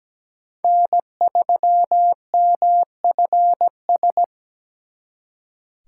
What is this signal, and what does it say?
Morse code, heard as a keyed tone, about 20 wpm, 700 Hz. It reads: N3MFS
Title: Speed Racing - USA Call Signs 17wpm